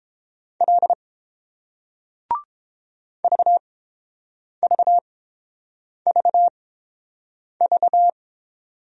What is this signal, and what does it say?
Morse code, heard as a keyed tone, about 30 wpm, spaced as for 7 wpm, 700 Hz. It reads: L VVVV